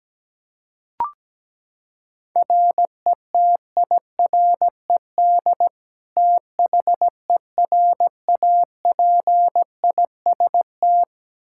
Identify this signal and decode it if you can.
Morse code, heard as a keyed tone, about 17 wpm, 700 Hz. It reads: RETIRED THERAPIST
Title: Practice QSO - Exchange 3 - More Info - 17wpm